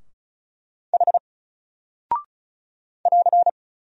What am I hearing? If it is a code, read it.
Morse code, heard as a keyed tone, about 34 wpm, 700 Hz. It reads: H <AR>
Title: Single Character (Letter:Number:Punctuation:Prosign) 35wpm